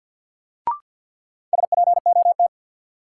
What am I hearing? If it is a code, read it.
Morse code, heard as a keyed tone, about 50 wpm, 700 Hz. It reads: SPOT
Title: Sentences from Top 1100 Words - New Word Drill - 50wpm